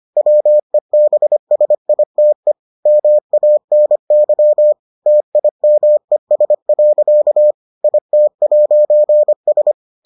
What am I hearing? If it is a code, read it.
Morse code, heard as a keyed tone, about 25 wpm, 600 Hz. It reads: WEBSITE MANY TIMES. IT'S